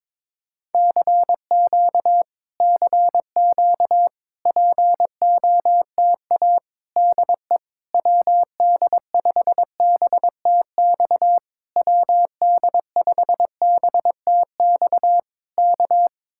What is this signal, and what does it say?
Morse code, heard as a keyed tone, about 21 wpm, 700 Hz. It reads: CQ CQ POTA DE WD5BTX WD5BTX K
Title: Speed-Racing Common QSO Phrases Encore - 22wpm